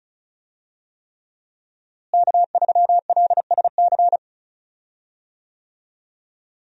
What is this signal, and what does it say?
Morse code, heard as a keyed tone, about 36 wpm, 700 Hz. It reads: K3LSC
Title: USA Call Signs - 5 Characters - Repeated 3x - 35wpm